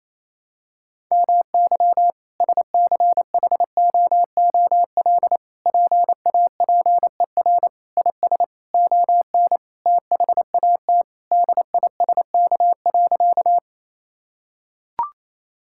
Morse code, heard as a keyed tone, about 28 wpm, 700 Hz. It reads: MY SCHOOL PAPER IS ON THAT DISK.
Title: Sentences from Top 1000 Words - Review Part 6 - 28wpm